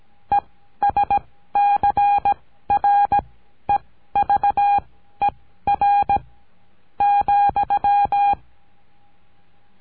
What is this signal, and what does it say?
Morse code, heard as a keyed tone, about 17 wpm, spaced as for 12 wpm, 810 Hz. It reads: ESCREVER,
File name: CURSO DE TELEGRAFIA